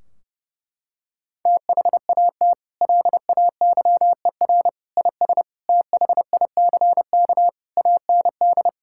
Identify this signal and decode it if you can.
Morse code, heard as a keyed tone, about 30 wpm, 700 Hz. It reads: THAT LAYER IS THICK AND
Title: Sentences from Top 1600 Words 30wpm